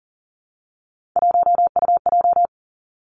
Morse code, heard as a keyed tone, about 39 wpm, 700 Hz. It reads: 1UJ
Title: Three Challenging Letter-Number Combinations 40wpm